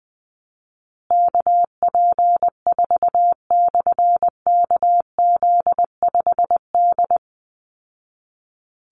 Morse code, heard as a keyed tone, about 20 wpm, 700 Hz. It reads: KP4/KZ5D
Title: Call Signs Repeated 3x - 20wpm